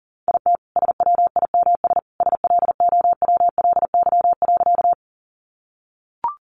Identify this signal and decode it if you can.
Morse code, heard as a keyed tone, about 40 wpm, 700 Hz. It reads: IT SWIMS SLOWLY.
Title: Additional Easy Phrases 40wpm